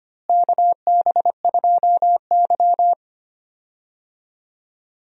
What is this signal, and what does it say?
Morse code, heard as a keyed tone, about 25 wpm, 700 Hz. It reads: KB2Y
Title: USA Call Signs - 4 Characters - Repeated 3x - 25wpm